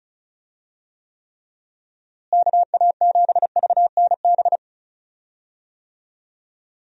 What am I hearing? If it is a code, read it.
Morse code, heard as a keyed tone, about 34 wpm, 700 Hz. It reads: KA7VNB
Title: USA Call Signs - 6 Characters - Repeated 3x - 35wpm